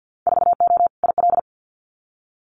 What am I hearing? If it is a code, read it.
Morse code, heard as a keyed tone, about 48 wpm, 700 Hz. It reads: 4O IL
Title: Winter Field Day Exchanges 50wpm